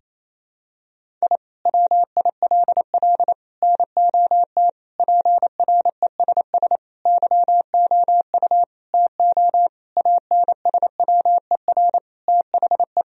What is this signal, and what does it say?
Morse code, heard as a keyed tone, about 28 wpm, 700 Hz. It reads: I WILL NOT PRESS YOU TO ANSWER THE
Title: Sentences from Top 1000 Words - Review Part 1 - 28wpm